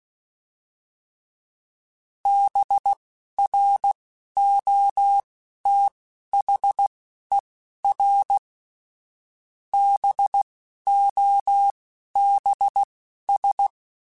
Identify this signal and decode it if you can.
Morse code, heard as a keyed tone, about 16 wpm, spaced as for 12 wpm, 800 Hz. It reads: BROTHER BOBS
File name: Misc / Misc-mp3s / BIG @10